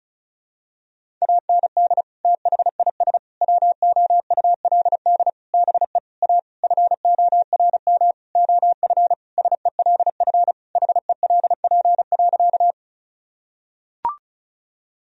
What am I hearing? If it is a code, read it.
Morse code, heard as a keyed tone, about 36 wpm, 700 Hz. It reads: AND THIS WOULD BE A FORM OF SELF HELP.
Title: Sentences from Top 1000 Words - Review Part 1 - 35wpm